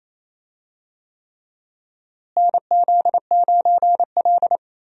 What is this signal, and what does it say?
Morse code, heard as a keyed tone, about 28 wpm, 700 Hz. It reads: NZ9L
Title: USA Call Signs - 4 Characters - Repeated 3x - 28wpm